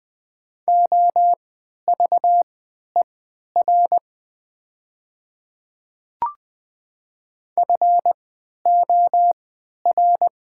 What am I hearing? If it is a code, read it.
Morse code, heard as a keyed tone, about 20 wpm, spaced as for 11 wpm, 700 Hz. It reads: OVER FOR
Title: 3x Character Spacing - Top 100 Words - 20wpm